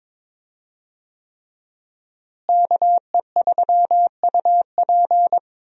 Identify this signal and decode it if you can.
Morse code, heard as a keyed tone, about 23 wpm, 700 Hz. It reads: KE3UP
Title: USA Call Signs - 5 Characters - Repeated 3x - 22wpm